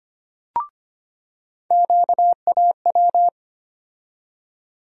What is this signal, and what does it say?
Morse code, heard as a keyed tone, about 25 wpm, 700 Hz. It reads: QAW
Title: USA Call Sign Postfixes - 3 Letters 25wpm